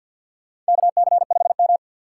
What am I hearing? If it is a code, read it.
Morse code, heard as a keyed tone, about 53 wpm, 700 Hz. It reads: KC5M
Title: USA Call Signs - 4 Characters - 3x Character Spacing - 50wpm